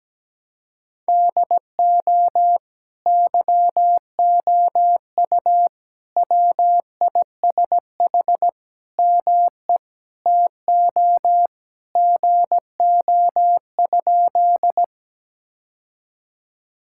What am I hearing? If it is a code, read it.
Morse code, heard as a keyed tone, about 17 wpm, 700 Hz. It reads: DO YOU WISH ME TO GO?
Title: Sentences from Top 1600 Words 17wpm